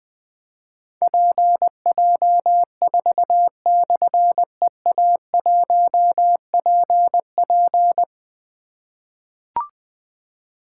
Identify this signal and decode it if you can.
Morse code, heard as a keyed tone, about 20 wpm, 700 Hz. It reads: PJ4/EA1PP
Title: Difficult Call Signs 20wpm